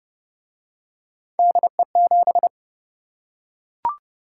Morse code, heard as a keyed tone, about 27 wpm, 700 Hz. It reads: DE7